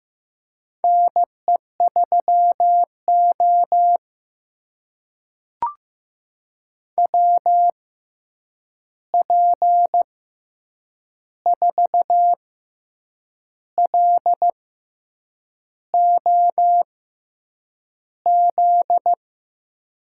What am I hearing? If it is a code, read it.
Morse code, heard as a keyed tone, about 15 wpm, spaced as for 7 wpm, 700 Hz. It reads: NE3O WP4LOZ
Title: USA Call Signs - 6x Character Spacing - 15wpm